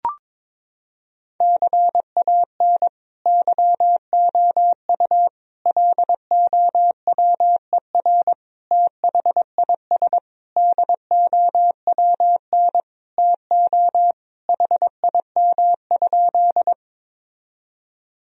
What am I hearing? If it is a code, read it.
Morse code, heard as a keyed tone, about 22 wpm, 700 Hz. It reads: CAN YOU LOWER THIS DOWN TO HIM?
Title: Sentences from Top 1000 Words - Review Part 4 - 22wpm